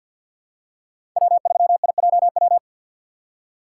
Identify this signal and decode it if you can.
Morse code, heard as a keyed tone, about 52 wpm, 700 Hz. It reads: W3IJW
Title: USA Call Signs - 5 Characters - 3x Character Spacing - 50wpm